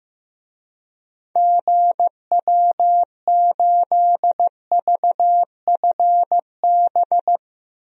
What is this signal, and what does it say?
Morse code, heard as a keyed tone, about 15 wpm, 700 Hz. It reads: GW8VFB
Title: DXCC-100 Call Signs - 6 Characters - 15wpm